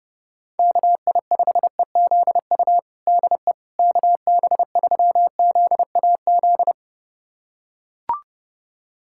Morse code, heard as a keyed tone, about 30 wpm, 700 Hz. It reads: KI5EZU DE KB3ZAZ
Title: USA Callsign DE Callsign - 6 Characters - 30wpm